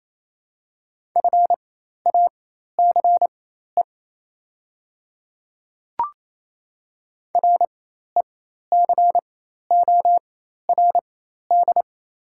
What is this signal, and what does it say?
Morse code, heard as a keyed tone, about 27 wpm, spaced as for 13 wpm, 690 Hz. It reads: FACE RECORD
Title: 4x Character Spacing - Top 500 Words - 28wpm